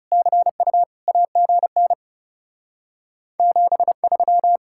Morse code, heard as a keyed tone, about 36 wpm, 700 Hz. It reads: CU AGN 73
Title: Speed-Racing Common QSO Phrases - 30wpm